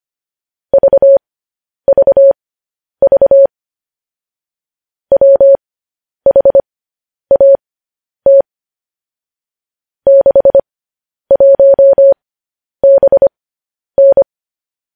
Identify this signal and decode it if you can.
Morse code, heard as a keyed tone, about 25 wpm, spaced as for 10 wpm, 550 Hz. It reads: VVV WHAT 61BN
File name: podcast / News headlines-1x-25x10